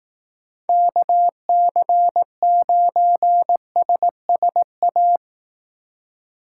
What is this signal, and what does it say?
Morse code, heard as a keyed tone, about 18 wpm, 700 Hz. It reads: KC9SSA
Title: Speed Racing - USA Call Signs 15wpm